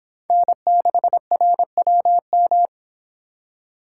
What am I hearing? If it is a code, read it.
Morse code, heard as a keyed tone, about 26 wpm, 700 Hz. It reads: N6RWM